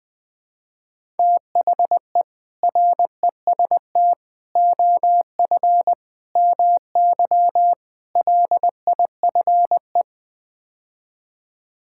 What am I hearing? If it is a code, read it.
Morse code, heard as a keyed tone, about 20 wpm, 700 Hz. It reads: THE REST OF MY LIFE
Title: Build A Phrase - 5-Word Phrases - 20wpm